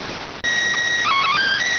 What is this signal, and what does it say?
SSTV (slow-scan television), PD 290.